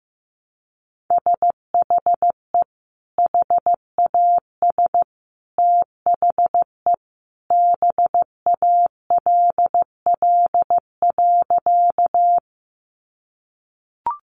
Morse code, heard as a keyed tone, about 15 wpm, 700 Hz. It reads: SHE HAS THE BALL.
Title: Sentences from Top 900 Words 15wpm